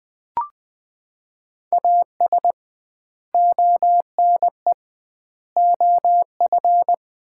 Morse code, heard as a keyed tone, about 20 wpm, 700 Hz. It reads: AS ONE OF
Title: 2x Word-Spacing - Sets of 5 Words - 20wpm